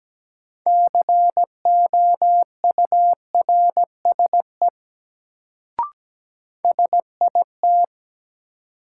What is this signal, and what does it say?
Morse code, heard as a keyed tone, about 17 wpm, 700 Hz. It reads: COURSE SIT
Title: Sentences from Top 500 Words - New Word Drill - 17wpm